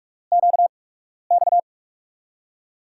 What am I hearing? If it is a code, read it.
Morse code, heard as a keyed tone, about 46 wpm, 700 Hz. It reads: Q X
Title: USA Call Signs - 4 Characters - 8x Character Spacing - 45wpm